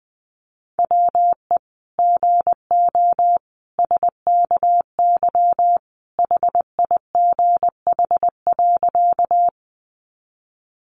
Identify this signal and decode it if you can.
Morse code, heard as a keyed tone, about 20 wpm, 700 Hz. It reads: WE GO SKY HIGH.
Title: Sentences from Top 1200 Words 20wpm